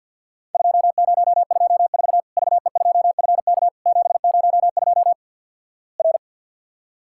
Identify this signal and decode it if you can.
Morse code, heard as a keyed tone, about 51 wpm, 690 Hz. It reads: 2024 VE2FK 702 R
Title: Worked All Europe DX Contest - QTC Exchange with Repeats - 50wpm